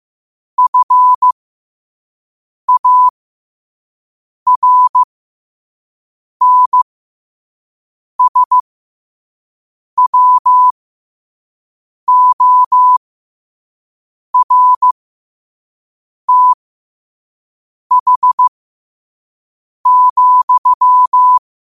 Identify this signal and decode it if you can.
Morse code, heard as a keyed tone, about 15 wpm, spaced as for 5 wpm, 1000 Hz. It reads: FARNSWORTH,
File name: qotdfiles / 5wpm